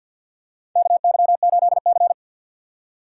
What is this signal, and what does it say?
Morse code, heard as a keyed tone, about 53 wpm, 700 Hz. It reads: KY8C
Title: USA Call Signs - 4 Characters - 2x Character Spacing - 50wpm